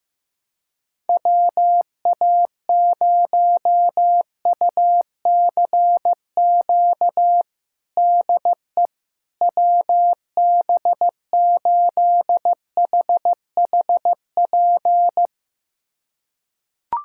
Morse code, heard as a keyed tone, about 15 wpm, 700 Hz. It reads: WA0UCQ DE WB8HHP
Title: USA Callsign DE Callsign - 6 Characters - 15wpm